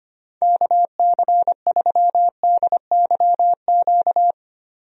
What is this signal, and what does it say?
Morse code, heard as a keyed tone, about 25 wpm, 700 Hz. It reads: KC3DYQ
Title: USA Call Signs - 6 Characters - 25wpm